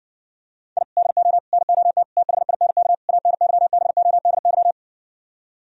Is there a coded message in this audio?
Yes: Morse code from 0.8 to 4.7 s, heard as a keyed tone, about 60 wpm, 700 Hz. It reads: I DO NOT THINK ANYBODY